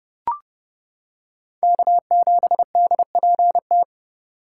Morse code, heard as a keyed tone, about 29 wpm, 700 Hz. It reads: K7DPT